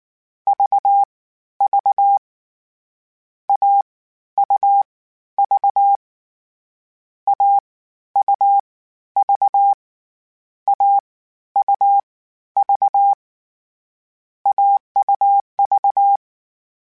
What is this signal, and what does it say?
Morse code, heard as a keyed tone, about 19 wpm, spaced as for 12 wpm, 800 Hz. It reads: VV AUV AUV AUV AUV
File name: Difficult Combos / Difficult Combos-mp3s / AUV ShortRun @18